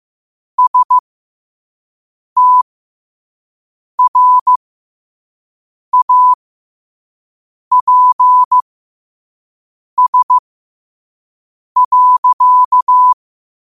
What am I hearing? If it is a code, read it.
Morse code, heard as a keyed tone, about 15 wpm, spaced as for 5 wpm, 1000 Hz. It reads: STRAPS.
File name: qotdfiles / 5wpm